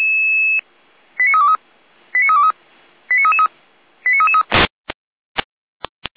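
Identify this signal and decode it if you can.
Five-tone selective call (ZVEI), heard as quick runs of five tones, tones of about 80 ms. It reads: 89323 89323 89333 89333
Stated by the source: ZVEI five-tone selective call 89330 received on an SDR.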